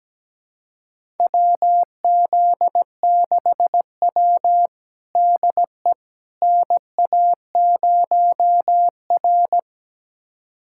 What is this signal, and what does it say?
Morse code, heard as a keyed tone, about 17 wpm, 700 Hz. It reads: WZ6W DE NA0R
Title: USA Callsign DE Callsign - 4 Characters - 17wpm